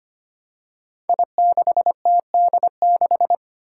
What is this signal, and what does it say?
Morse code, heard as a keyed tone, about 25 wpm, 700 Hz. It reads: I6TD6
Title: Five Challenging Letter-Number Combinations 25wpm